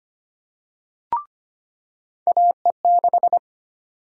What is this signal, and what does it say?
Morse code, heard as a keyed tone, about 25 wpm, 700 Hz. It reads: AE6
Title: USA Call Sign Prefixes - 25wpm